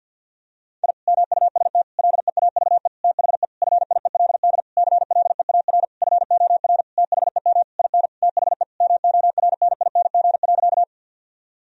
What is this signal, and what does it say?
Morse code, heard as a keyed tone, about 52 wpm, 700 Hz. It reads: I MUST LEAVE THE FIELD CLEAR FOR THEM IN THE MORNING.
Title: Sentences from Top 1000 Words - Review Part 2 - 50wpm